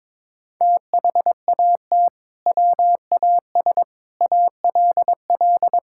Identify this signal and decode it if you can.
Morse code, heard as a keyed tone, about 22 wpm, 700 Hz. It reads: THAT WAS ALL
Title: Sentences from Top 1600 Words 22wpm